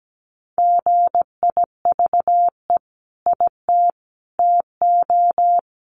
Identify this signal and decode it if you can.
Morse code, heard as a keyed tone, about 17 wpm, 700 Hz. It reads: GIVE IT TO
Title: Sentences from Top 200 Words 17wpm